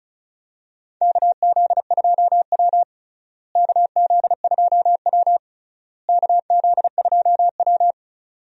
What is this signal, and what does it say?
Morse code, heard as a keyed tone, about 36 wpm, 700 Hz. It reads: KZ2W KZ2W KZ2W
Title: USA Call Signs - 4 Characters - Repeated 3x - 35wpm